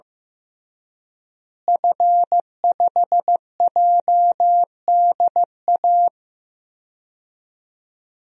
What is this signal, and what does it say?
Morse code, heard as a keyed tone, about 15 wpm, 700 Hz. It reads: F5JDA
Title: DXCC-100 Call Signs - 5 Characters - Repeated 3x - 15wpm